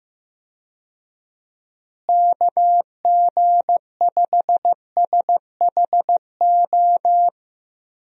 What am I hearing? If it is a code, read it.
Morse code, heard as a keyed tone, about 15 wpm, 700 Hz. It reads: KG5SHO
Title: USA Call Signs - 6 Characters - Repeated 2x - 15wpm